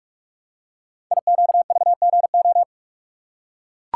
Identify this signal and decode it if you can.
Morse code, heard as a keyed tone, about 44 wpm, 700 Hz. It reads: IQVGO
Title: Code Groups - 5 Letters - 45wpm